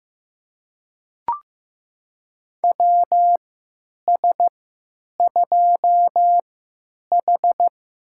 Morse code, heard as a keyed tone, about 15 wpm, spaced as for 9 wpm, 700 Hz. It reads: WS2H
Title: USA Call Signs - 4 Characters - 3x Character Spacing - 15wpm